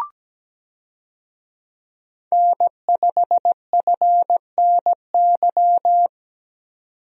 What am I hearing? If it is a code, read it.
Morse code, heard as a keyed tone, about 17 wpm, 700 Hz. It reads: N5FNY
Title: USA Call Signs - 5 Characters - 17wpm